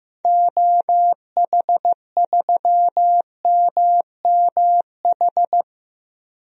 Morse code, heard as a keyed tone, about 15 wpm, 700 Hz. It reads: OH3MMH
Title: DXCC-100 Call Signs - 6 Characters - 15wpm